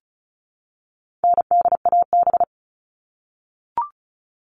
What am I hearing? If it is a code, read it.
Morse code, heard as a keyed tone, about 34 wpm, 700 Hz. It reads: NDAB
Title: All Cut Numbers - 4-Digit 35wpm